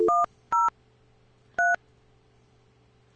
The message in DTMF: 103